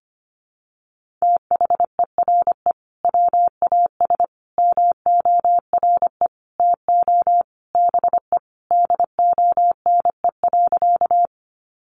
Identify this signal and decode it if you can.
Morse code, heard as a keyed tone, about 25 wpm, 700 Hz. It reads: THERE WAS MORE TO BE DONE.
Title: Sentences from Top 500 Words 25wpm